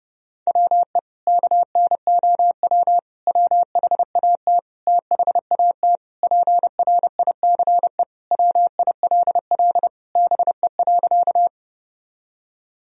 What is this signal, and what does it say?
Morse code, heard as a keyed tone, about 30 wpm, 700 Hz. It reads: WE KNOW WHAT THAT PRICE WILL BE.
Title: Sentences from Top 1000 Words - Review Part 4 - 30wpm